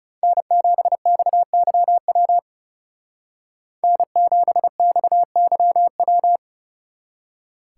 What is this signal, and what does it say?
Morse code, heard as a keyed tone, about 39 wpm, 700 Hz. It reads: N7XYW N7XYW